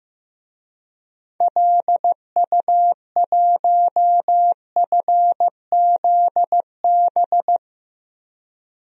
Morse code, heard as a keyed tone, about 15 wpm, 700 Hz. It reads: LU1FZB
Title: DXCC-100 Call Signs - 6 Characters - 15wpm